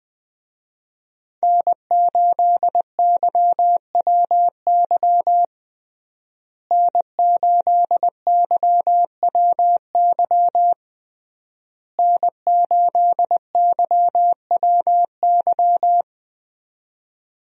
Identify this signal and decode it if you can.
Morse code, heard as a keyed tone, about 20 wpm, 700 Hz. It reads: N8YWY N8YWY N8YWY
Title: USA Call Signs - 5 Characters - Repeated 3x - 20wpm